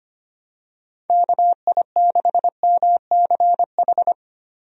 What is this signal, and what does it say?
Morse code, heard as a keyed tone, about 25 wpm, 700 Hz. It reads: KI6MCH
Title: USA Call Signs - 6 Characters - Repeated 2x - 25wpm